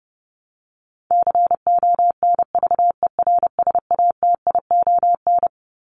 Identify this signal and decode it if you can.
Morse code, heard as a keyed tone, about 28 wpm, 700 Hz. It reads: CONVERSATION
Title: Top 1751-2000 Words 30wpm